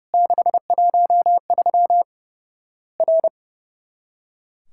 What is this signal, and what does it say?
Morse code, heard as a keyed tone, about 32 wpm, 700 Hz. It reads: 613 R
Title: Worked All Europe DX Contest - QTC Exchange - 30wpm